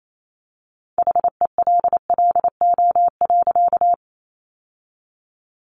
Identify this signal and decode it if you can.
Morse code, heard as a keyed tone, about 28 wpm, 700 Hz. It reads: HELLO.
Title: Sentences from Top 1100 Words 28wpm